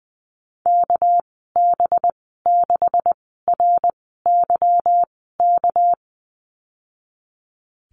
Morse code, heard as a keyed tone, about 20 wpm, spaced as for 15 wpm, 700 Hz. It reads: KB6RYK
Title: 200 Lesson 32g - Letter k - Call Signs 20wpm